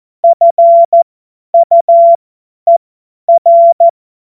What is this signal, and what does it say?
Morse code, heard as a keyed tone, about 14 wpm, spaced as for 10 wpm, 680 Hz. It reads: FUER
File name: Morsekurs / Witze / Witz 28. September 2021, 50 F70